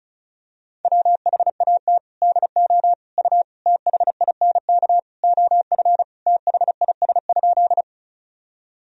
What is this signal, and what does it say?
Morse code, heard as a keyed tone, about 36 wpm, 700 Hz. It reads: WHAT DO U THINK OF THIS?
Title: Sentences from Top 200 Words and Abbreviations 35wpm